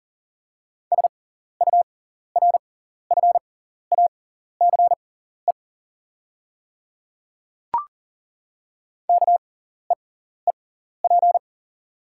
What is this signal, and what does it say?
Morse code, heard as a keyed tone, about 40 wpm, spaced as for 14 wpm, 690 Hz. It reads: SURFACE KEEP
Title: 6x Character Spacing - Top 500 Words - 40wpm